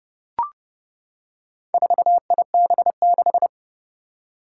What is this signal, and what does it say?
Morse code, heard as a keyed tone, about 30 wpm, 700 Hz. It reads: 4IB6
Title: Four Challenging Letter-Number Combinations 30wpm